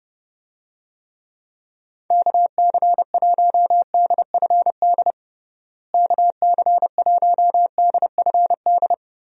Morse code, heard as a keyed tone, about 30 wpm, 700 Hz. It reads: KC1DFD KC1DFD
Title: USA Call Signs - 6 Characters - Repeated 2x - 30wpm